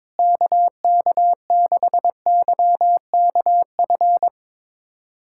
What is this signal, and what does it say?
Morse code, heard as a keyed tone, about 22 wpm, 700 Hz. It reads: KK6YKF
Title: USA Callsign DE Callsign 22wpm